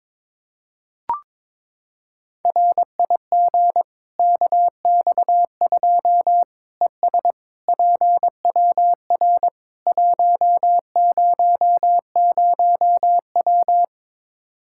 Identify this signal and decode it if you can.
Morse code, heard as a keyed tone, about 22 wpm, 700 Hz. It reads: RIG KX2 ES PWR 100W
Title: Practice QSO - Exchange 2 - Info - 22wpm